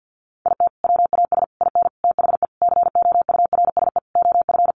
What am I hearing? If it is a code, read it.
Morse code, heard as a keyed tone, about 50 wpm, 700 Hz. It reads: IT WAS IN THE COURSE OF
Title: Sentences from Top 1400 Words 50wpm